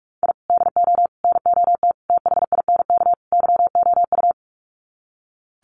Morse code, heard as a keyed tone, about 44 wpm, 700 Hz. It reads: I DO NOT THINK YOU
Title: Sets of 5 Words 45wpm